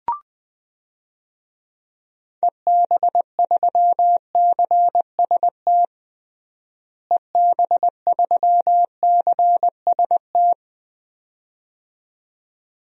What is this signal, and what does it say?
Morse code, heard as a keyed tone, about 21 wpm, 700 Hz. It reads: EB3CST EB3CST
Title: DXCC-100 Call Signs - 6 Characters - Repeated 2x - 20wpm